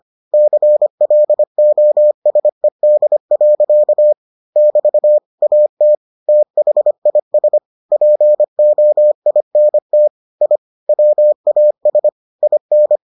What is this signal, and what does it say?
Morse code, heard as a keyed tone, about 25 wpm, 600 Hz. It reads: CLOSED. <BT> AT THIS POINT I WAS IN